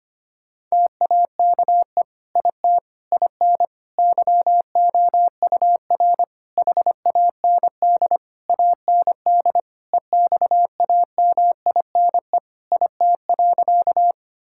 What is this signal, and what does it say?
Morse code, heard as a keyed tone, about 25 wpm, 700 Hz. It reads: TAKE IT IN YOUR HAND AND EXAMINE IT.
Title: Sentences from Top 1600 Words 25wpm